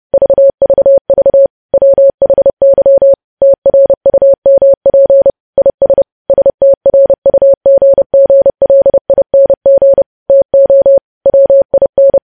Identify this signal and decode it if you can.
Morse code, heard as a keyed tone, about 30 wpm, 550 Hz. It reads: VVV WHY TRUMP IS STRUGGLING TO WIN
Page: news headlines in morse code at 30 wpm